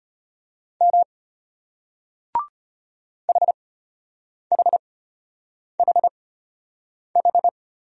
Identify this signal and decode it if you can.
Morse code, heard as a keyed tone, about 33 wpm, spaced as for 8 wpm, 700 Hz. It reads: M HHHH